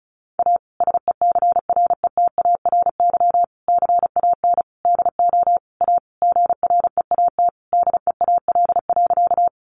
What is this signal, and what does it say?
Morse code, heard as a keyed tone, about 34 wpm, 700 Hz. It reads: A SECRETARY CAN DO A GREAT DEAL.
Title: Sentences from Top 1300 Words 35wpm